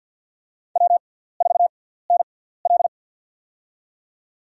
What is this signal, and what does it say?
Morse code, heard as a keyed tone, about 52 wpm, spaced as for 17 wpm, 700 Hz. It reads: W4NL
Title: USA Call Signs - 4 Characters - 6x Character Spacing - 50wpm